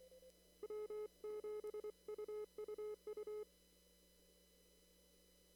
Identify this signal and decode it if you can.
Morse code, heard as a keyed tone, about 24 wpm, 420 Hz. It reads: W7UUU
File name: CW / kt5j wpx cw-op-k5pi